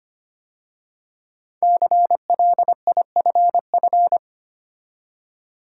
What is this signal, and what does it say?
Morse code, heard as a keyed tone, about 25 wpm, 700 Hz. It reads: CLIFF
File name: 5 Letter Words - First 3 Letters Given - 25wpm